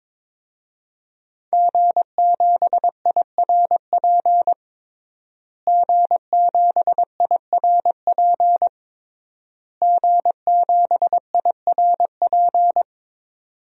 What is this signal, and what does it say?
Morse code, heard as a keyed tone, about 22 wpm, 700 Hz. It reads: G7IRP G7IRP G7IRP
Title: DXCC-100 Call Signs - 5 Characters - Repeated 3x - 22wpm